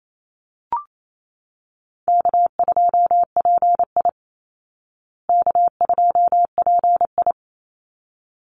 Morse code, heard as a keyed tone, about 28 wpm, 700 Hz. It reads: K2PI K2PI
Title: Easy Call Signs Repeated 2x - 28wpm